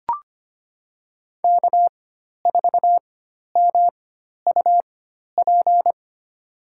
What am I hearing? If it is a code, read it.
Morse code, heard as a keyed tone, about 25 wpm, spaced as for 12 wpm, 700 Hz. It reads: K4MUP
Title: USA Call Signs - 5 Characters - 4x Character Spacing - 25wpm